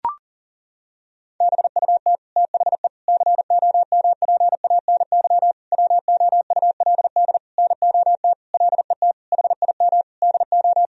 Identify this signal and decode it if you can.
Morse code, heard as a keyed tone, about 40 wpm, 700 Hz. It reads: BUT THE COMPANY WOULD NOT LET HIM DO
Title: Sentences from Top 1000 Words - Review Part 3 - 40wpm